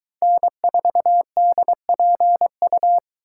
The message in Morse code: N4DPU

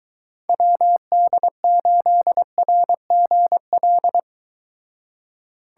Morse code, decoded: WD8RGL

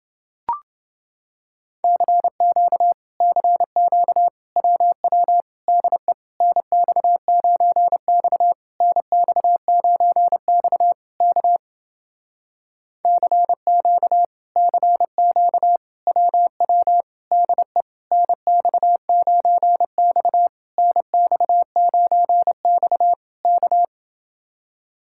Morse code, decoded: CQ CQ WW DE NX9X NX9X K CQ CQ WW DE NX9X NX9X K

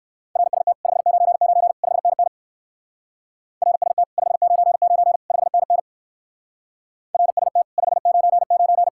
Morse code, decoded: RST 599 5NN RST 599 5NN RST 599

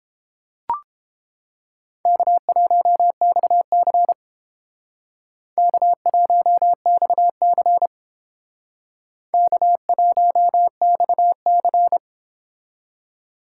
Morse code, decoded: K1XC K1XC K1XC

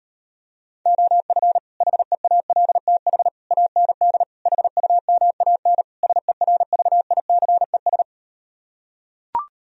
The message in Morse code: OF HEALTH AND HUMAN SERVICES

